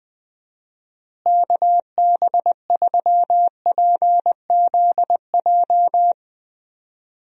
Morse code, heard as a keyed tone, about 21 wpm, 700 Hz. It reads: KB3PZJ
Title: USA Call Signs - 6 Characters - Repeated 2x - 20wpm